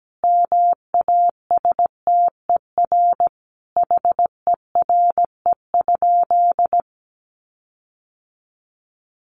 Morse code, heard as a keyed tone, about 17 wpm, 700 Hz. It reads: MASTER HERE?
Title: Sentences from Top 1300 Words 17wpm